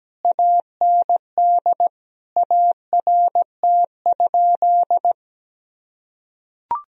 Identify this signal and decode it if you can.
Morse code, heard as a keyed tone, about 17 wpm, 700 Hz. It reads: AND ART?
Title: Sentences from Top 1000 Words - Review Part 3 - 17wpm